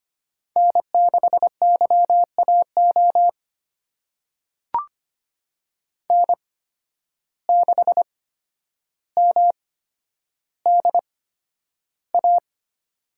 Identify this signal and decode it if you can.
Morse code, heard as a keyed tone, about 25 wpm, spaced as for 11 wpm, 700 Hz. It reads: N6YAO N6MDA
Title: USA Call Signs - 5 Characters - 8x Character Spacing - 25wpm